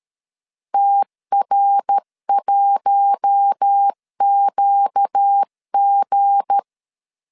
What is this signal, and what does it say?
Morse code, heard as a keyed tone, about 13 wpm, 800 Hz. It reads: TR1QG
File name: LEZIONI / 7 GF9Z / 7 G 12wpm 2055